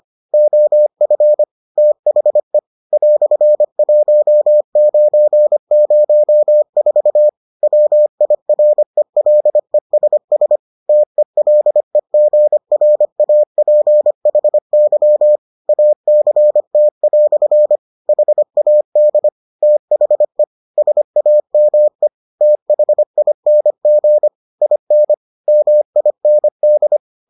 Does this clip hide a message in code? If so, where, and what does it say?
Yes: Morse code from 0.3 to 27.0 s, heard as a keyed tone, about 25 wpm, 600 Hz. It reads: OF THE "1904 WIRELESS TELEGRAPHY ACT" HAD THE SAME THING IN MIND